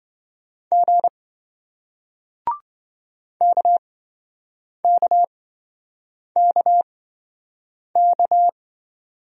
Morse code, heard as a keyed tone, about 26 wpm, spaced as for 7 wpm, 700 Hz. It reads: G KKKK